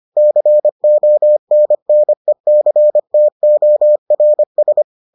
Morse code, heard as a keyed tone, about 25 wpm, 600 Hz. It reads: CONNECTORS